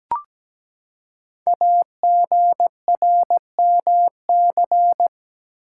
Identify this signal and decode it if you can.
Morse code, heard as a keyed tone, about 17 wpm, 700 Hz. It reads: AGRMC
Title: Code Groups - 5 Letters - 17wpm